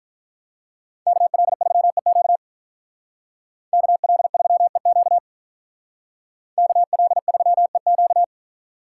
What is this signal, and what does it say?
Morse code, heard as a keyed tone, about 45 wpm, 700 Hz. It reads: KL3EQ KL3EQ KL3EQ